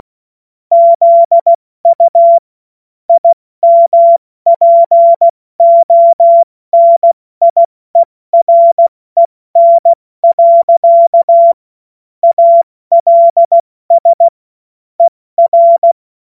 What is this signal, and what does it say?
Morse code, heard as a keyed tone, about 16 wpm, 680 Hz. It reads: ZU IMPONIEREN. ALS ER
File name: Morsekurs / Witze / Witz 10. Mai 2022 70 F80